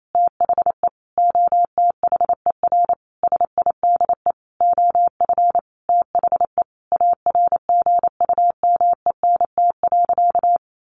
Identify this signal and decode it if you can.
Morse code, heard as a keyed tone, about 28 wpm, 700 Hz. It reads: THE OTHER SIDE OF THE ARGUMENT.
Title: Sentences from Top 1400 Words 28wpm